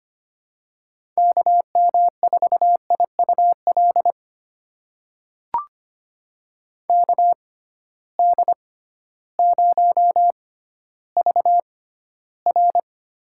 Morse code, heard as a keyed tone, about 25 wpm, spaced as for 13 wpm, 700 Hz. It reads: KM4IUL KD0VR
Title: USA Call Signs - 6 Characters - 6x Character Spacing - 25wpm